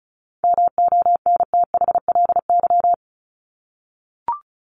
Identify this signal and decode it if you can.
Morse code, heard as a keyed tone, about 34 wpm, 700 Hz. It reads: MONTHLY
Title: LY Suffix - Top 100 Words with Base Words 35wpm